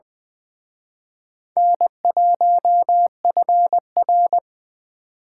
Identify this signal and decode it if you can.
Morse code, heard as a keyed tone, about 24 wpm, 700 Hz. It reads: N1FR
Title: Speed Racing - USA 1x2 Call Signs 20wpm